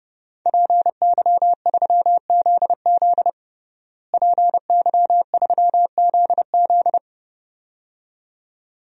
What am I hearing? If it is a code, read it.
Morse code, heard as a keyed tone, about 30 wpm, 700 Hz. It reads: PY3ZZ PY3ZZ
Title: DXCC-100 Call Signs - 5 Characters - Repeated 2x - 30wpm